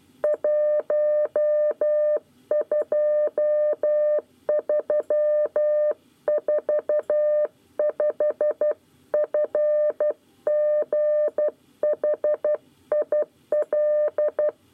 Morse code, heard as a keyed tone, about 12 wpm, 600 Hz. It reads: 12345 FGHIL
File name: 9 lezione cw 1